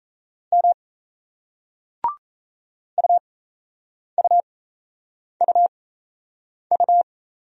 Morse code, heard as a keyed tone, about 38 wpm, spaced as for 8 wpm, 700 Hz. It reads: M UUUU